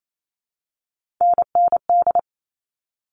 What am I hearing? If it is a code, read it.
Morse code, heard as a keyed tone, about 28 wpm, 700 Hz. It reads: NND
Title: All Cut Numbers - 3-Digit 28wpm